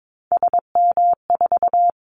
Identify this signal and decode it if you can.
Morse code, heard as a keyed tone, about 22 wpm, 700 Hz. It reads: SM4
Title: Callsign Prefixes for Top 25 DXCC - 22wpm